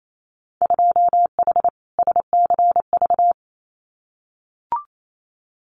Morse code, heard as a keyed tone, about 28 wpm, 700 Hz. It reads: 2H SCV
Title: Winter Field Day Exchanges Encore 28wpm